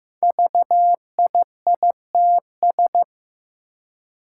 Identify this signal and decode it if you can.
Morse code, heard as a keyed tone, about 15 wpm, 700 Hz. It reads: VIITS
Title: Five Challenging Letter-Number Combinations 15wpm